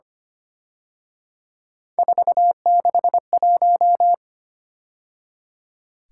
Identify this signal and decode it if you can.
Morse code, heard as a keyed tone, about 29 wpm, 700 Hz. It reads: 461
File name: Speed-Racing - Three Letter-Number Combinations - 25wpm